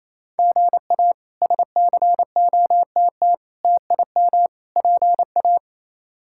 Morse code, heard as a keyed tone, about 28 wpm, 700 Hz. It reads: GA SCOTT TIM PA
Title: K1USN SST QSO Practice - 28wpm